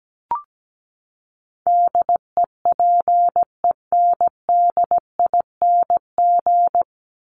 Morse code, heard as a keyed tone, about 17 wpm, 700 Hz. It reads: DEPENDING
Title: ING Suffix - Top 100 Words with Base Words 17wpm